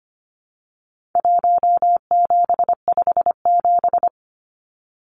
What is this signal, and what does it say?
Morse code, heard as a keyed tone, about 25 wpm, 700 Hz. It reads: 1757
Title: Numbers - Contest Style - 4-Digit 25wpm